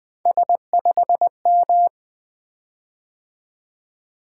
Morse code, heard as a keyed tone, about 21 wpm, 700 Hz. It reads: S5M
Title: Three Letter-Number Combinations - 20wpm